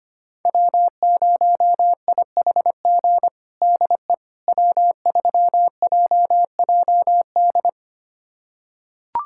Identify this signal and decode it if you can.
Morse code, heard as a keyed tone, about 25 wpm, 700 Hz. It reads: W0IHG DE W3JJD
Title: USA Callsign DE Callsign - 5 Characters - 25wpm